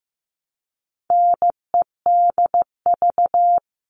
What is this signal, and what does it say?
Morse code, heard as a keyed tone, about 15 wpm, 700 Hz. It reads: NEDV
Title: All Cut Numbers - 4-Digit 15wpm